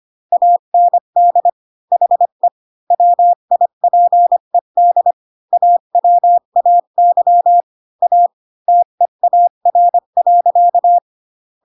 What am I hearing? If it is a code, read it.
Morse code, heard as a keyed tone, about 25 wpm, 700 Hz. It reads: AND HE WIPED AWAY A TEAR.
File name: Pinocchio / Pinocchio ch3 20 25 ik1hge